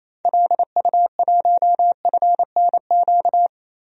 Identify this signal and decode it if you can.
Morse code, heard as a keyed tone, about 28 wpm, 700 Hz. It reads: LU1FNQ
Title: DXCC-100 Call Signs - 6 Characters - Repeated 2x - 28wpm